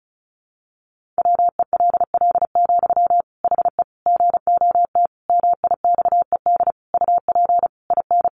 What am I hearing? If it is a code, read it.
Morse code, heard as a keyed tone, about 34 wpm, 700 Hz. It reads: WELL, HE GOT MIXED UP IN
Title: Sentences from Top 1500 Words 35wpm